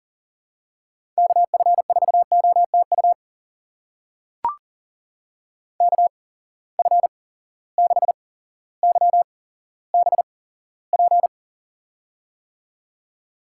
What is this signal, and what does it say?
Morse code, heard as a keyed tone, about 42 wpm, spaced as for 17 wpm, 700 Hz. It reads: KF4OTU KF6YBP
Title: USA Call Signs - 6 Characters - 8x Character Spacing - 40wpm